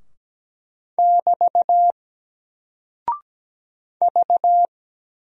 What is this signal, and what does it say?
Morse code, heard as a keyed tone, about 17 wpm, 700 Hz. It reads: <BT> V
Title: Single Character (Letter:Number:Punctuation:Prosign) Alt 17wpm